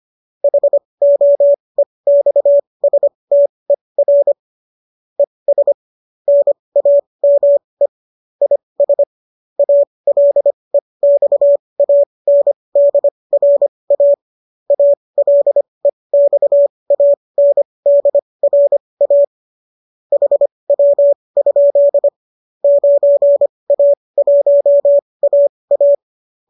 Morse code, heard as a keyed tone, about 25 wpm, spaced as for 20 wpm, 570 Hz. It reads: HOEXSTER ES NAME IS ALEXANDRA ALEXANDRA HW? 9A1AA
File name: qso219 20